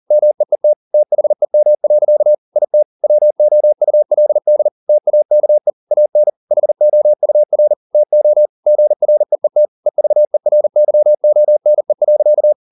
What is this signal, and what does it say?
Morse code, heard as a keyed tone, about 40 wpm, 600 Hz. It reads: MEET THEM. IT WOULD TAKE AN HOUR TO GREET EVERYONE.